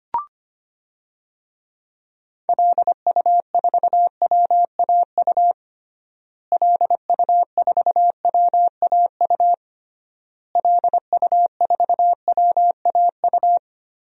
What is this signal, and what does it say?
Morse code, heard as a keyed tone, about 25 wpm, 700 Hz. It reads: LU4WAU LU4WAU LU4WAU
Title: DXCC-100 Call Signs - 6 Characters - Repeated 3x - 25wpm